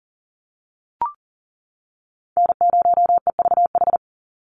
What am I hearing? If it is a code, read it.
Morse code, heard as a keyed tone, about 40 wpm, 700 Hz. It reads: N0EVH
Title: NA SOTA Chaser Callsigns 40wpm